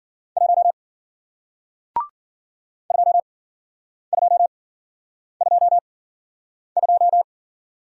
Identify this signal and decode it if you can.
Morse code, heard as a keyed tone, about 54 wpm, spaced as for 9 wpm, 700 Hz. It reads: 1 2222